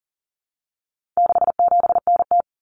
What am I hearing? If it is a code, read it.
Morse code, heard as a keyed tone, about 40 wpm, 700 Hz. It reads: 67NT
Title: Common Cut Numbers - 4-Digit 40wpm